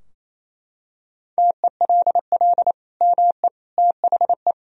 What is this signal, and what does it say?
Morse code, heard as a keyed tone, about 28 wpm, 700 Hz. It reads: TELL ME THE
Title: Sentences from Top 1000 Words - Review Part 4 - 28wpm